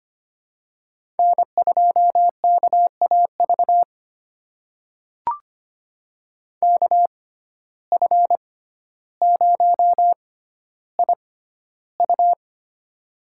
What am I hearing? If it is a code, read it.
Morse code, heard as a keyed tone, about 25 wpm, spaced as for 13 wpm, 700 Hz. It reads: N2KAV KF0IU
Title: USA Call Signs - 5 Characters - 6x Character Spacing - 25wpm